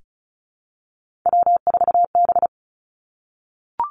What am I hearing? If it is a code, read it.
Morse code, heard as a keyed tone, about 36 wpm, 700 Hz. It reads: W4B
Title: Three Challenging Letter-Number Combinations 35wpm